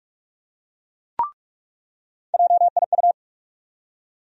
Morse code, heard as a keyed tone, about 46 wpm, 700 Hz. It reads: JIU
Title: USA Call Sign Postfixes - 3 Letters 45wpm